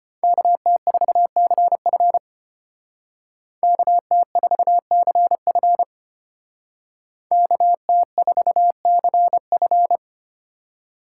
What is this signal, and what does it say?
Morse code, heard as a keyed tone, about 31 wpm, 700 Hz. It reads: KT4CF KT4CF KT4CF
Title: Speed Racing - USA Call Signs 25wpm